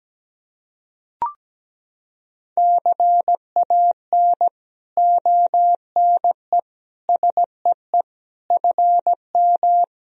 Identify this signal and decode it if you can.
Morse code, heard as a keyed tone, about 17 wpm, 700 Hz. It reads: CAN ONE SEE FM
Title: Sentences from Top 100 Words and Abbreviations 17wpm